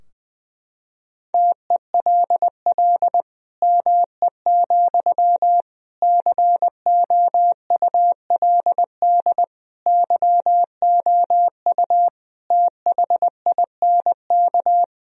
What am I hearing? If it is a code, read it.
Morse code, heard as a keyed tone, about 21 wpm, 700 Hz. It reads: TELL ME, COULD YOU THINK
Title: Sentences from Top 1000 Words - Review Part 2 - 20wpm